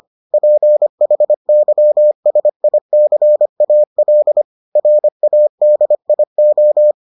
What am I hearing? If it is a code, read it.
Morse code, heard as a keyed tone, about 25 wpm, 600 Hz. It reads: PHYSICAL RADIO